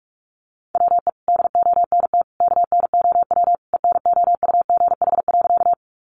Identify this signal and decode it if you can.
Morse code, heard as a keyed tone, about 46 wpm, 700 Hz. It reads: WE DONT KNOW ENOUGH.
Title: Sentences from Top 400 Words 45wpm